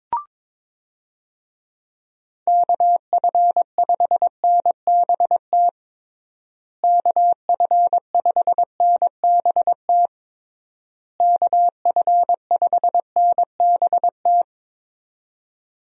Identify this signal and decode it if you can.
Morse code, heard as a keyed tone, about 22 wpm, 700 Hz. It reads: KF5NBT KF5NBT KF5NBT
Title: USA Call Signs - 6 Characters - Repeated 3x - 22wpm